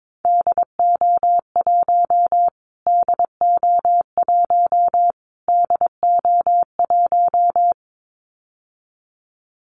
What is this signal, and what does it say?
Morse code, heard as a keyed tone, about 22 wpm, 700 Hz. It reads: DO1 DO1 DO1
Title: Callsign Prefixes for Top 25 DXCC - Repeated 3x - 22wpm